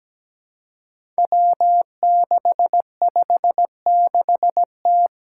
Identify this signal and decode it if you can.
Morse code, heard as a keyed tone, about 17 wpm, 700 Hz. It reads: W656T
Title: Five Challenging Letter-Number Combinations 17wpm